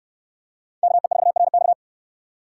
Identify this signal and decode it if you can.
Morse code, heard as a keyed tone, about 69 wpm, 700 Hz. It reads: KE4RX